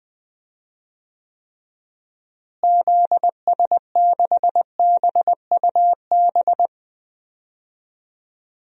Morse code, heard as a keyed tone, about 20 wpm, 700 Hz. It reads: ZS6BUB
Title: DXCC-100 Call Signs - 6 Characters - Repeated 2x - 20wpm